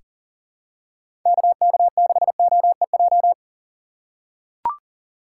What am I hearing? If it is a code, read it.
Morse code, heard as a keyed tone, about 39 wpm, 700 Hz. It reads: KK6OEJ
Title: USA Call Signs - 6 Characters - 6x Character Spacing - 40wpm